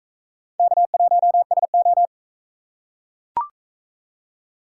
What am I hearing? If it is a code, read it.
Morse code, heard as a keyed tone, about 42 wpm, 700 Hz. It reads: K1SO